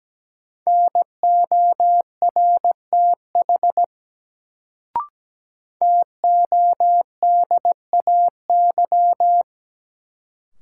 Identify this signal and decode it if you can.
Morse code, heard as a keyed tone, about 17 wpm, 700 Hz. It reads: NORTH TODAY
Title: Sets of 5 Words Encore - New Word Drill - 17wpm